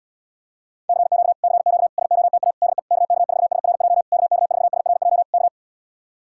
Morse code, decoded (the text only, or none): CQ CQ SOTA DE KK4SRJ KK4SRJ K